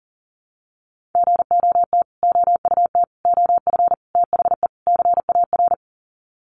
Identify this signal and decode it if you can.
Morse code, heard as a keyed tone, about 40 wpm, 700 Hz. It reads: GOT OUT OF THE CAR
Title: Sets of 5 Words Encore 40wpm